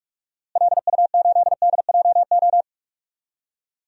Morse code, heard as a keyed tone, about 41 wpm, 700 Hz. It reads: LU8DJO
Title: DXCC-100 Call Signs - 6 Characters - Repeated 3x - 45wpm